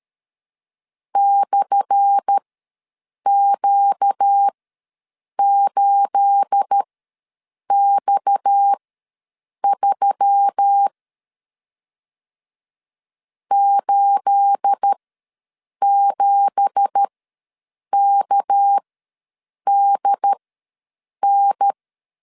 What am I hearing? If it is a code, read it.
Morse code, heard as a keyed tone, about 12 wpm, spaced as for 7 wpm, 800 Hz. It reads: /Q8X3 87KDN
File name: LEZIONI / 10 SIGNS / 10A SLASH ws 12wpm 3443